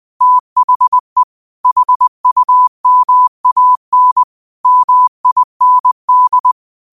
Morse code, heard as a keyed tone, about 20 wpm, 1000 Hz. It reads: THE HUMAN MIND
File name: qotdfiles / 20wpm